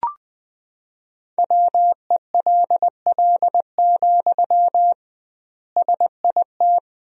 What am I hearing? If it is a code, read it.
Morse code, heard as a keyed tone, about 20 wpm, 700 Hz. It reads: WELL, SIT
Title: 2x Word-Spacing - Sentences from Top 500 Words - 20wpm